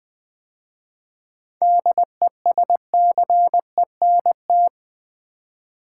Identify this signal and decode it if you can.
Morse code, heard as a keyed tone, about 20 wpm, 690 Hz. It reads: DESCENT
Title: 7 Letter Words - First 3 Letters Given - 20wpm